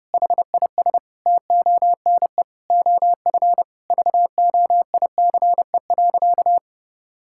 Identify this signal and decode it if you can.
Morse code, heard as a keyed tone, about 30 wpm, 700 Hz. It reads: HIS TONE OF VOICE.
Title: Sentences from Top 1600 Words 30wpm